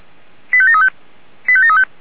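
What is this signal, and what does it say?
Five-tone selective call (ZVEI), heard as quick runs of five tones, tones of about 70 ms. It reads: 85735 85735
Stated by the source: ZVEI five-tone selective call received on SDR.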